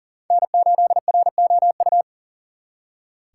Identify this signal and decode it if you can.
Morse code, heard as a keyed tone, about 53 wpm, spaced as for 47 wpm, 700 Hz. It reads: N8ROU
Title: Speed Racing - USA 1x3 Call Signs 40wpm